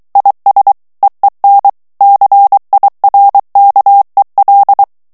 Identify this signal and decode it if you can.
Morse code, heard as a keyed tone, about 23 wpm, 800 Hz. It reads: IS EEN CIRKEL
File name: HOLLANDAIS texte clair 1 vitesse 22 mots minute